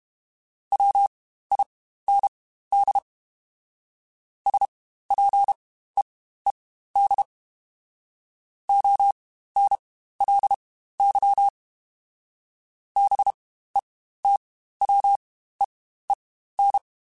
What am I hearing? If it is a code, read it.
Morse code, heard as a keyed tone, about 32 wpm, spaced as for 15 wpm, 800 Hz. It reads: WIND SPEED ONLY BETWEEN